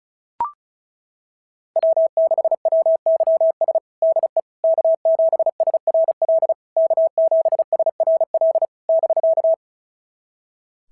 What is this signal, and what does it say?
Morse code, heard as a keyed tone, about 36 wpm, 660 Hz. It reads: W6WYS DE K7SRL K7SRL <BK>
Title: SOTA QSO Practice 35wpm